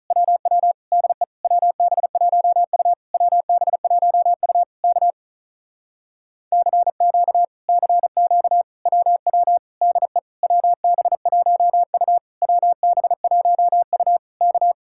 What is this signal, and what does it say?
Morse code, heard as a keyed tone, about 43 wpm, 700 Hz. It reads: WW DE WB1U WB1U K CQ CQ WW DE WB1U WB1U K
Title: Speed-Racing Common QSO Phrases - 35wpm